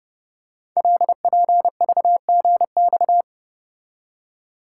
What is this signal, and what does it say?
Morse code, heard as a keyed tone, about 28 wpm, 700 Hz. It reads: LPVGX
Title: Code Groups - 5 Letters - 30wpm